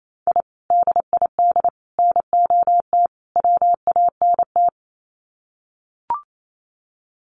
Morse code, heard as a keyed tone, about 28 wpm, 700 Hz. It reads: I DID NOT WANT
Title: Sets of 4 Words 28wpm